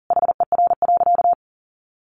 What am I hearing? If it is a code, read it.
Morse code, heard as a keyed tone, about 40 wpm, 700 Hz. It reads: HER.
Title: Sentences from Top 1100 Words 40wpm